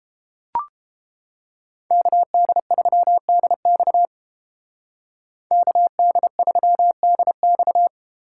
Morse code, heard as a keyed tone, about 29 wpm, 700 Hz. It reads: KD3DX KD3DX